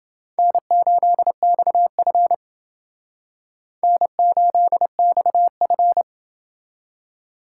Morse code, heard as a keyed tone, about 29 wpm, 700 Hz. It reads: N8XF N8XF